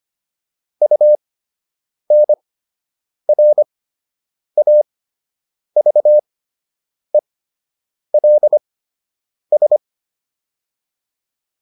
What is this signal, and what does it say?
Morse code, heard as a keyed tone, about 25 wpm, spaced as for 8 wpm, 600 Hz. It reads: UNRAVELS